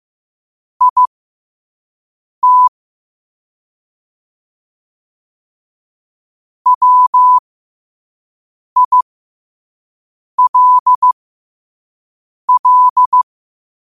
Morse code, heard as a keyed tone, about 15 wpm, spaced as for 5 wpm, 1000 Hz. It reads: IT WILL
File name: qotdfiles / 5wpm